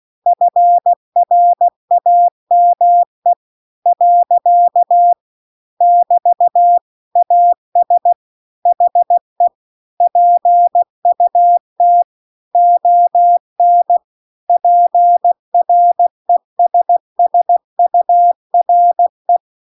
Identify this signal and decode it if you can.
Morse code, heard as a keyed tone, about 16 wpm, 700 Hz. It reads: FRAME . <BT> AS HE PUT ON PRESSURE